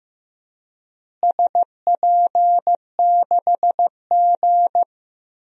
Morse code, heard as a keyed tone, about 16 wpm, 700 Hz. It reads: SP6G
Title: DXCC-100 Call Signs - 4 Characters - Repeated 2x - 15wpm